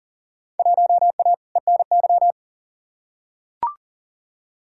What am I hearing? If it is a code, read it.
Morse code, heard as a keyed tone, about 37 wpm, 700 Hz. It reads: 1A ENY
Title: Field Day Exchanges Encore - 40wpm